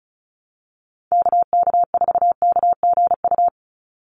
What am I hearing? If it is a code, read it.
Morse code, heard as a keyed tone, about 34 wpm, 700 Hz. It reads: KK4KGU
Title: USA Call Signs 35wpm